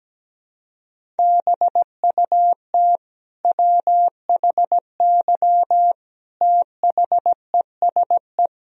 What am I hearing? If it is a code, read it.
Morse code, heard as a keyed tone, about 17 wpm, 700 Hz. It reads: BUT WHY THESE
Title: Sentences from Top 1000 Words - Review Part 1 - 17wpm